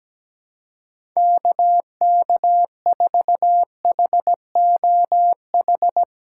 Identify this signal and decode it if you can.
Morse code, heard as a keyed tone, about 17 wpm, 700 Hz. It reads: KK4HOH
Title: USA Call Signs - 6 Characters - 2x Character Spacing - 17wpm